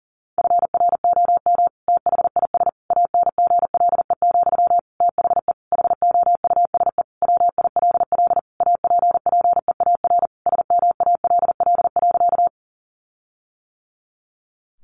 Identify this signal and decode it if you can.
Morse code, heard as a keyed tone, about 40 wpm, 700 Hz. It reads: FROM THIS ANGLE, THE HOUSE WILL APPEAR SMALL.
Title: Sentences from Top 1500 Words 40wpm